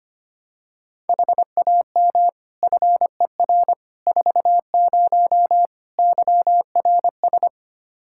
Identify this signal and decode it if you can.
Morse code, heard as a keyed tone, about 25 wpm, 700 Hz. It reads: HAM FER 40 YRS
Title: Practice QSO - Exchange 3 - More Info - 25wpm